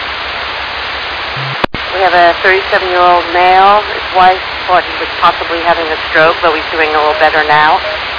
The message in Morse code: none

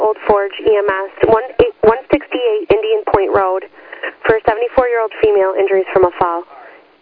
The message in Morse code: none

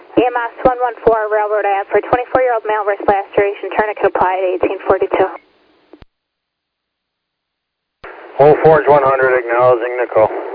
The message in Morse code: none

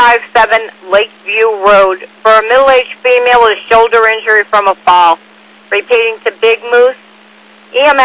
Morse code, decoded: none

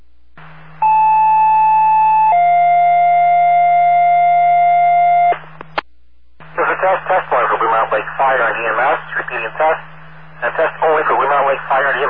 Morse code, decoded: none

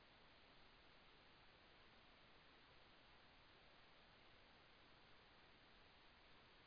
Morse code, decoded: none

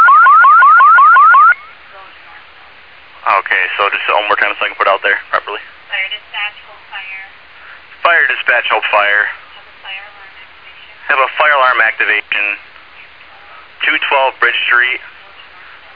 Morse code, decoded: none

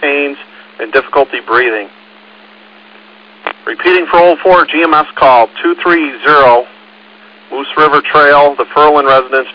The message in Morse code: none

